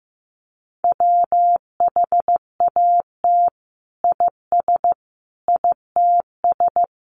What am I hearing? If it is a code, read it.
Morse code, heard as a keyed tone, about 15 wpm, 700 Hz. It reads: WHAT IS ITS
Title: Sentences from Top 1400 Words 15wpm